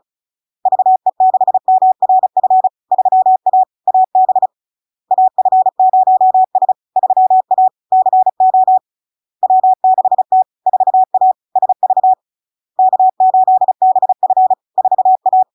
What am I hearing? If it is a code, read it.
Morse code, heard as a keyed tone, about 36 wpm, 750 Hz. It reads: VE6MRF 3A AB AF0S 3A CO W6T 4A SV K8BF 4A
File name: LARK-FD-2022-Log-in-CW-35wpm-750-Hz